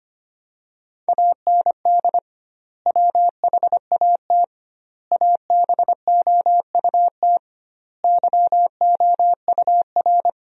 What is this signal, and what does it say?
Morse code, heard as a keyed tone, about 25 wpm, 700 Hz. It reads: AND WHAT ABOUT YOUR
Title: 2x Word-Spacing - Sentences from Top 200 Words - 25wpm